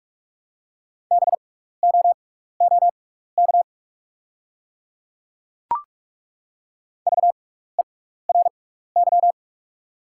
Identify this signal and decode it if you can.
Morse code, heard as a keyed tone, about 48 wpm, spaced as for 16 wpm, 690 Hz. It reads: BOOK VERY
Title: 6x Character Spacing - Top 300 Words - 45wpm